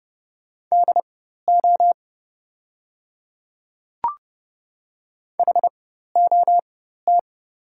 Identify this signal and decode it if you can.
Morse code, heard as a keyed tone, about 30 wpm, spaced as for 14 wpm, 700 Hz. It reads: DO HOT
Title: 4x Character Spacing - Top 100 Words - 30wpm